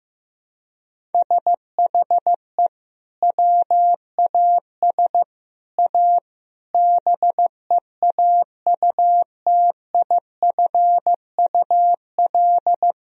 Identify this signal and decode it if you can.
Morse code, heard as a keyed tone, about 16 wpm, 700 Hz. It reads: SHE WAS A BEAUTIFUL
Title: Sentences from Top 1000 Words - Review Part 1 - 15wpm